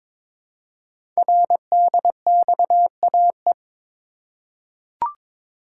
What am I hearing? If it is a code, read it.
Morse code, heard as a keyed tone, about 21 wpm, 700 Hz. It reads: RDXAE
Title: Code Groups - 5 Letters - 22wpm